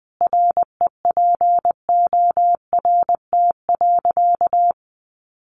Morse code, heard as a keyed tone, about 20 wpm, 700 Hz. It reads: REPORT.
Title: Sentences from Top 1300 Words 20wpm